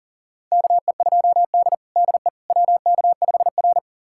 Morse code, heard as a keyed tone, about 40 wpm, 700 Hz. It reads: KE2D DE WK5R
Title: USA Callsign DE Callsign - 4 Characters - 40wpm